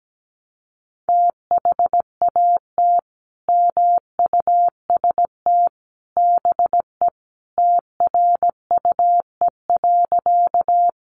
Morse code, heard as a keyed tone, about 17 wpm, 700 Hz. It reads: THAT MUST BE TRUE.
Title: Sentences from Top 400 Words 17wpm